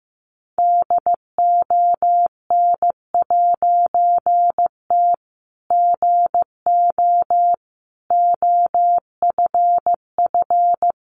Morse code, heard as a keyed tone, about 15 wpm, 700 Hz. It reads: DON'T GO OFF
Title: A Sound Of Thunder 15wpm